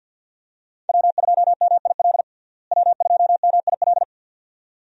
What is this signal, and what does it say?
Morse code, heard as a keyed tone, about 52 wpm, 700 Hz. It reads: W2MIL W2MIL
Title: USA Call Signs - 5 Characters - Repeated 2x - 50wpm